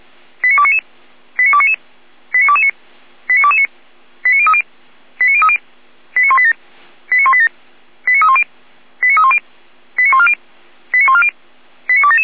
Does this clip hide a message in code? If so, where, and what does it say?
Yes: five-tone selective call (ZVEI) from 0.4 to 12.3 s, heard as quick runs of five tones, tones of about 70 ms. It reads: 89299 89299 89229 89229 89030 89030 89187 89187 89310 89310 89140 89140 89277